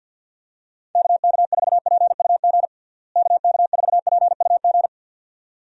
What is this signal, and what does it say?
Morse code, heard as a keyed tone, about 45 wpm, 700 Hz. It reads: KK4PUG KK4PUG
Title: USA Call Signs - 6 Characters - Repeated 2x - 50wpm